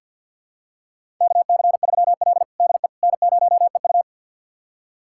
Morse code, heard as a keyed tone, about 50 wpm, 700 Hz. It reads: KC3L DE N0EU
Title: USA Callsign DE Callsign - 4 Characters - 50wpm